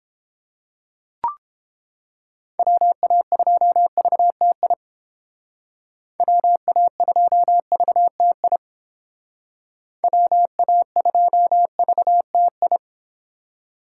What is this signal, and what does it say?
Morse code, heard as a keyed tone, about 30 wpm, 700 Hz. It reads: WA2VTI WA2VTI WA2VTI